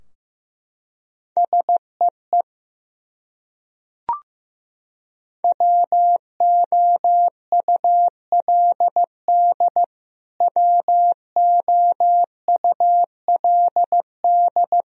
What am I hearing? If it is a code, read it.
Morse code, heard as a keyed tone, about 15 wpm, 700 Hz. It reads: SEE WOULD WOULD
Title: Top 100 Words Repeated 15wpm